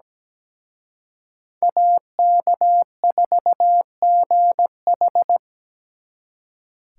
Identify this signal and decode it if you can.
Morse code, heard as a keyed tone, about 20 wpm, 700 Hz. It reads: AK4GH
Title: Speed Racing - USA 2x2 Call Signs 17wpm